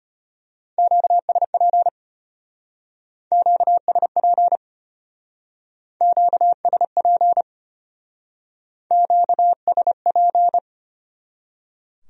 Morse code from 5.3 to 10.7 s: QSP QSP

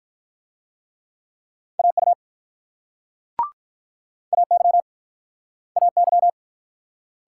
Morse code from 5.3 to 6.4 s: AY

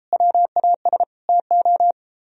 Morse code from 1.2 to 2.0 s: TO

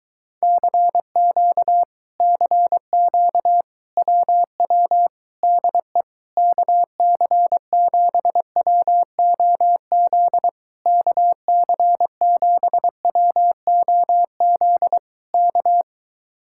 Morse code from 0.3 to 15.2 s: CQ CQ WW DE KC7WOZ KC7WOZ

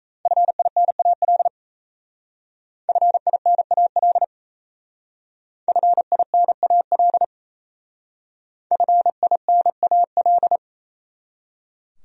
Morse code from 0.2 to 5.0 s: FINAL FINAL

